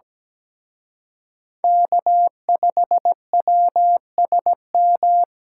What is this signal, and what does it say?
Morse code, heard as a keyed tone, about 20 wpm, 700 Hz. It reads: K5WSM
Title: Speed Racing - USA 1x3 Call Signs 17wpm